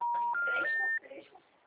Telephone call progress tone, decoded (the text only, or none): special information tone